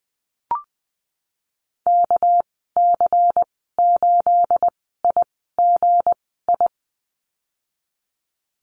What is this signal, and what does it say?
Morse code, heard as a keyed tone, about 20 wpm, spaced as for 15 wpm, 700 Hz. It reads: KC8IGI
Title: 219 Lesson 35g - Number 0 - Call Signs 20wpm